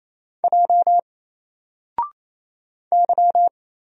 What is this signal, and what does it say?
Morse code, heard as a keyed tone, about 28 wpm, 700 Hz. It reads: J Y
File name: Single-Letter-Number-28wpm